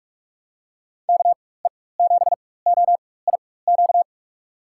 Morse code, heard as a keyed tone, about 46 wpm, spaced as for 21 wpm, 700 Hz. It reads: KE7OIQ
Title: USA Call Signs - 4x Character Spacing - 45wpm